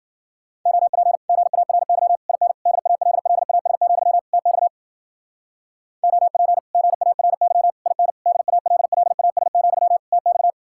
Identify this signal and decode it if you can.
Morse code, heard as a keyed tone, about 52 wpm, 700 Hz. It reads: OP GARY IN DALLAS, TX OP GARY IN DALLAS, TX